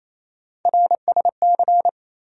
Morse code, heard as a keyed tone, about 28 wpm, 700 Hz. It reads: RSC
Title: USA Call Sign Postfixes - 3 Letters 28wpm